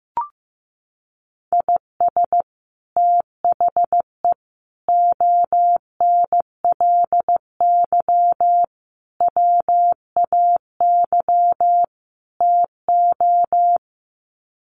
Morse code, heard as a keyed tone, about 15 wpm, 700 Hz. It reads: IS THE ONLY WAY TO
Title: Sets of 5 Words Encore 15wpm